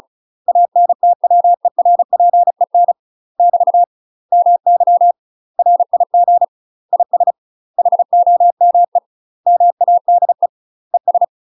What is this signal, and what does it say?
Morse code, heard as a keyed tone, about 36 wpm, 700 Hz. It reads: ANTWERPEN <BT> MY RIG IS HOME MADE ES
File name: UK MASTER QSOs TRAINING / 35WPM / UK MASTER QSOs 0035